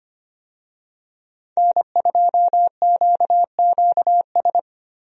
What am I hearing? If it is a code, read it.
Morse code, heard as a keyed tone, about 25 wpm, 700 Hz. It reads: N2QQS
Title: USA Call Signs - 5 Characters - 25wpm